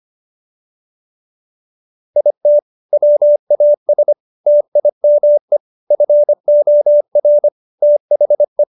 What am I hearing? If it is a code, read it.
Morse code, heard as a keyed tone, about 25 wpm, 590 Hz. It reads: IT WAS TIME FOR THE